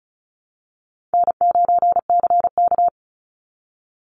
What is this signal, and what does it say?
Morse code, heard as a keyed tone, about 34 wpm, 700 Hz. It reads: N9CK
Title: Easy Call Signs Repeated 3x - 35wpm